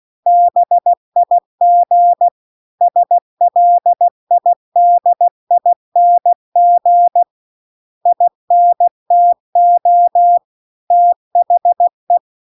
Morse code, heard as a keyed tone, about 16 wpm, 700 Hz. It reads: BIG SLIDING INTO THE
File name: CW-The-Big-Twoheaded-River-2-16WPM